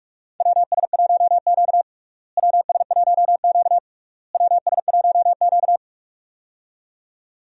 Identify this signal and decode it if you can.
Morse code, heard as a keyed tone, about 42 wpm, 700 Hz. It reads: WS1Q WS1Q WS1Q
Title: USA Call Signs - 4 Characters - Repeated 3x - 45wpm